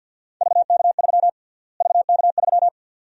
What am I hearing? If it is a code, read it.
Morse code, heard as a keyed tone, about 52 wpm, 700 Hz. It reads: VK3 VK3
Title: USA Call Sign Prefixes - Repeated 2x - 50wpm